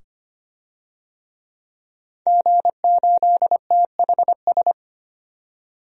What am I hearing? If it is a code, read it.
Morse code, heard as a keyed tone, about 25 wpm, 700 Hz. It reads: G8THS
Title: DXCC-100 Call Signs - 5 Characters - 25wpm